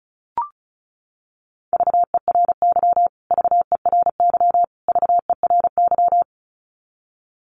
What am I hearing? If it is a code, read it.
Morse code, heard as a keyed tone, about 34 wpm, 700 Hz. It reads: VERY VERY VERY
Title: Top 200 Words Repeated 35wpm